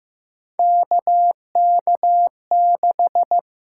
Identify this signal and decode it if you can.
Morse code, heard as a keyed tone, about 15 wpm, 700 Hz. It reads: KK6
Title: USA Call Signs - 6 Characters - 6x Character Spacing - 15wpm